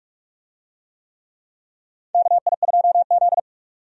Morse code, heard as a keyed tone, about 46 wpm, 700 Hz. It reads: KI2Z
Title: USA Call Signs - 4 Characters - 45wpm